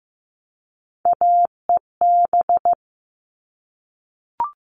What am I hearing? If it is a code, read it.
Morse code, heard as a keyed tone, about 15 wpm, 700 Hz. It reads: AEB
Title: All Cut Numbers - 3-Digit 15wpm